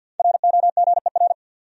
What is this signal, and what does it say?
Morse code, heard as a keyed tone, about 52 wpm, 700 Hz. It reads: ROGER
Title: USA Common Names - Max 5 Letters - 50wpm